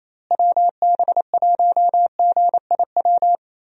Morse code, heard as a keyed tone, about 28 wpm, 700 Hz. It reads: WB1GIW
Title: USA Call Signs - 6 Characters - Repeated 3x - 28wpm